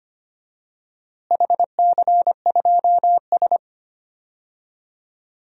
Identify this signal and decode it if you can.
Morse code, heard as a keyed tone, about 25 wpm, 700 Hz. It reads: HC2S
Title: DXCC-100 Call Signs - 4 Characters - Repeated 3x - 25wpm